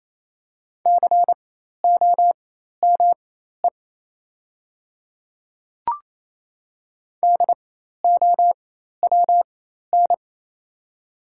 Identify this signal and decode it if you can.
Morse code, heard as a keyed tone, about 27 wpm, spaced as for 13 wpm, 690 Hz. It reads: COME DOWN
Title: 4x Character Spacing - Top 100 Words - 28wpm